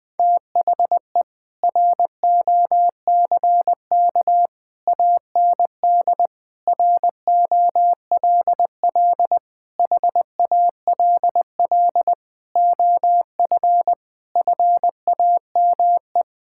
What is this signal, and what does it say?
Morse code, heard as a keyed tone, about 20 wpm, 700 Hz. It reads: THE ROCK AND ROLL HALL OF FAME
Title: Sentences with 7-Word Phrases - Spoken Beginning - 20wpm